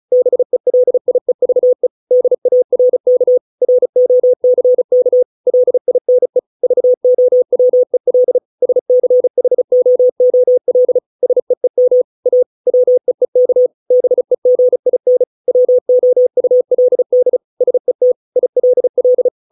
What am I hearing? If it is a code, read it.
Morse code, heard as a keyed tone, about 36 wpm, 500 Hz. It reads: BELIEVE DARK ROCK LINE VOWEL SCHOOL SEEM A WEEK BEGIN WOULD SET ILL